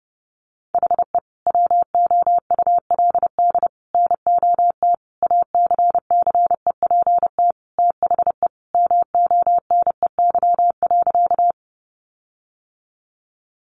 Morse code, heard as a keyed tone, about 30 wpm, 700 Hz. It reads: HE WOULD NOT ACCEPT THE MONEY.
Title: Sentences from Top 1300 Words 30wpm